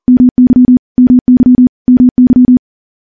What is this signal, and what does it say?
Morse code, heard as a keyed tone, about 40 wpm, 250 Hz. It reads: MY MY MY
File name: morse-training-40WPM